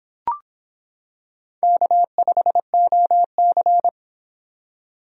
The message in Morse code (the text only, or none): K5OC